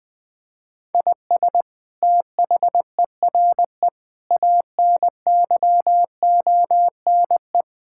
Morse code, heard as a keyed tone, about 20 wpm, 700 Hz. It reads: IS THERE ANYONE
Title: Sentences from Top 1000 Words - Review Part 6 - 20wpm